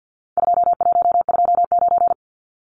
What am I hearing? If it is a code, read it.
Morse code, heard as a keyed tone, about 50 wpm, 700 Hz. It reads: 2129
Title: Numbers - 4-Digit 50wpm